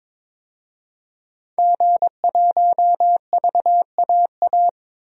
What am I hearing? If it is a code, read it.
Morse code, heard as a keyed tone, about 22 wpm, 700 Hz. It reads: G1VAA
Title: DXCC-100 Call Signs - 5 Characters - 22wpm